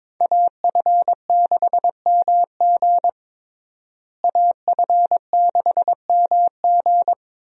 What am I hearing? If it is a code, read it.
Morse code, heard as a keyed tone, about 22 wpm, 700 Hz. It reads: AF6MG AF6MG
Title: USA Call Signs - 5 Characters - Repeated 3x - 22wpm